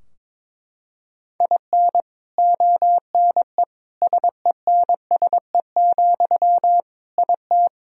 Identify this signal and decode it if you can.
Morse code, heard as a keyed tone, about 22 wpm, 700 Hz. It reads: IN ONE SENSE, IT
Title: Sentences from Top 1000 Words - Review Part 4 - 22wpm